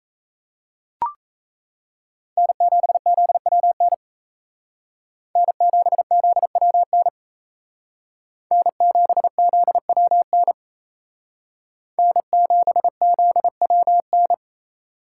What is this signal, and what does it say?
Morse code, heard as a keyed tone, about 33 wpm, 700 Hz. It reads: N7ZWN N7ZWN N7ZWN N7ZWN
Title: Speed Racing - USA 1x3 Call Signs 28wpm